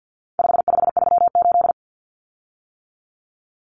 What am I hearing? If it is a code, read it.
Morse code, heard as a keyed tone, about 50 wpm, 700 Hz. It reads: 5538
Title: Common Cut Numbers - 4-Digit 50wpm